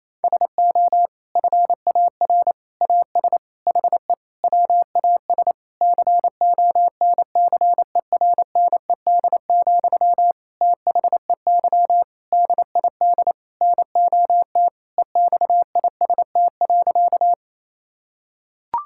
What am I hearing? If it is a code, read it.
Morse code, heard as a keyed tone, about 28 wpm, 700 Hz. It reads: SO FAR AS HE WAS CONCERNED, THEY DID NOT EXIST.
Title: Sentences from Top 1000 Words - Review Part 6 - 28wpm